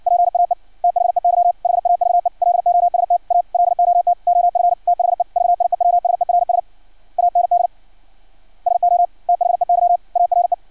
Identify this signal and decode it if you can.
Morse code, heard as a keyed tone, about 56 wpm, 700 Hz. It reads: PAE THEY HAVE LOST A LOT OF THE VIEWSERS AND SO THEY ARE
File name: archived / CRSnet